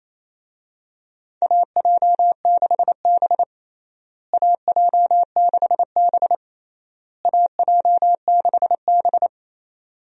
Morse code, heard as a keyed tone, about 28 wpm, 700 Hz. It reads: AJ6B AJ6B AJ6B
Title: USA Call Signs - 4 Characters - Repeated 3x - 28wpm